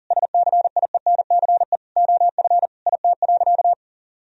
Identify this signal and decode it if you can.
Morse code, heard as a keyed tone, about 40 wpm, 700 Hz. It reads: SCIENCE OF IT.
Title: Sentences from Top 1000 Words - Review Part 2 - 40wpm